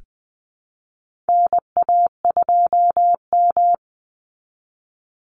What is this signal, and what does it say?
Morse code, heard as a keyed tone, about 20 wpm, 700 Hz. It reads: NA2M
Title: Call Signs Repeated 2x - 20wpm